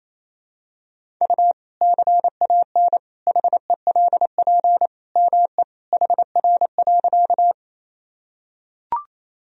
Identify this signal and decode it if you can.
Morse code, heard as a keyed tone, about 28 wpm, 700 Hz. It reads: U CAN HELP ME HR.
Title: Sentences from Top 200 Words and Abbreviations 28wpm